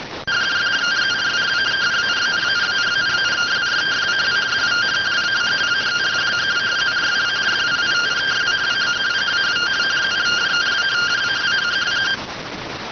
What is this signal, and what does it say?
Two-tone teleprinter signal, about 170 Hz shift.
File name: findings / S ss-portrait.gen-20 rx